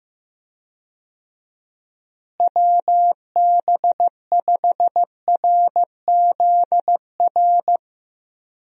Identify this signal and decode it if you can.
Morse code, heard as a keyed tone, about 15 wpm, 700 Hz. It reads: WB5RZR
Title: USA Call Signs - 6 Characters - Repeated 3x - 15wpm